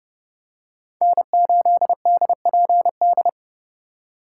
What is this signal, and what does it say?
Morse code, heard as a keyed tone, about 30 wpm, 700 Hz. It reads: N8DPD
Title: K1USN SST QSO Practice - 30wpm